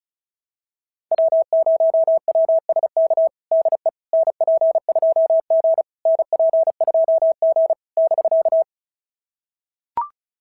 Morse code, heard as a keyed tone, about 34 wpm, 660 Hz. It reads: W0WSK DE NP2G NP2G <BK>
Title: SOTA QSO Practice 35wpm